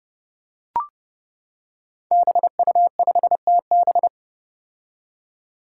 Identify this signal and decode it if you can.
Morse code, heard as a keyed tone, about 30 wpm, 700 Hz. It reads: BU5TB
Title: Five Challenging Letter-Number Combinations 30wpm